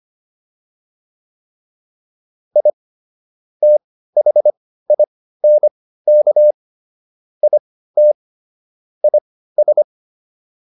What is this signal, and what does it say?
Morse code, heard as a keyed tone, about 25 wpm, spaced as for 15 wpm, 600 Hz. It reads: I THINK IT IS